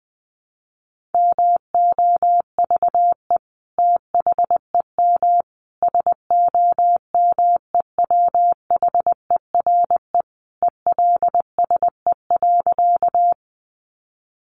Morse code, heard as a keyed tone, about 20 wpm, 700 Hz. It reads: MOVE THEM SOMEWHERE ELSE.
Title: Sentences from Top 1400 Words 20wpm